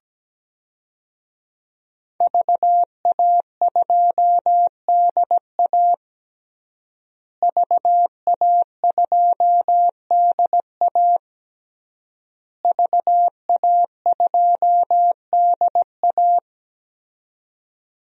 Morse code, heard as a keyed tone, about 17 wpm, 700 Hz. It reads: VA2DA VA2DA VA2DA
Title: DXCC-100 Call Signs - 5 Characters - Repeated 3x - 17wpm